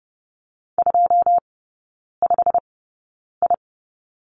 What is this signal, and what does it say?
Morse code, heard as a keyed tone, about 30 wpm, spaced as for 9 wpm, 700 Hz. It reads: 25I
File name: course / 30:10wpm / 087 Lesson 15e - Number 5 - Three Characters 30wpm